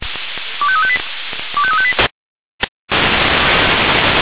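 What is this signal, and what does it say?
Five-tone selective call (ZVEI), heard as quick runs of five tones, tones of about 70 ms. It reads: 25358 25358
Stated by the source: ZVEI five-tone selective call received on an SDR.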